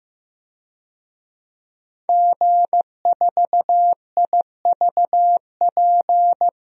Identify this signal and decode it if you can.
Morse code, heard as a keyed tone, about 15 wpm, 700 Hz. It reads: G4IVP
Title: DXCC-100 Call Signs - 5 Characters - Repeated 3x - 15wpm